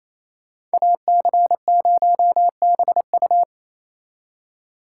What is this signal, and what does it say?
Morse code, heard as a keyed tone, about 36 wpm, 700 Hz. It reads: AC0BU